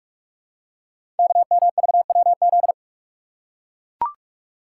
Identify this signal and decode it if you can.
Morse code, heard as a keyed tone, about 44 wpm, 690 Hz. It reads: KMVWZ
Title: Code Groups - 5 Letters - 45wpm